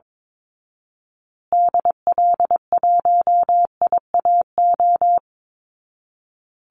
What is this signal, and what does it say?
Morse code, heard as a keyed tone, about 22 wpm, 700 Hz. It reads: DL1IAO
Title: Call Signs Repeated 2x - 22wpm